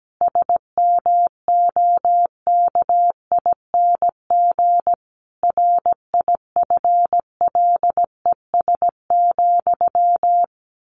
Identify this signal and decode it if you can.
Morse code, heard as a keyed tone, about 17 wpm, 700 Hz. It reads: SMOKING RIFLES,
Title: A Sound Of Thunder 17wpm